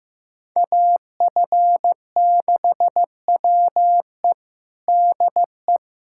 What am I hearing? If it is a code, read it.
Morse code, heard as a keyed tone, about 15 wpm, 700 Hz. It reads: AF6WE DE
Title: USA Callsign DE Callsign - 5 Characters - 15wpm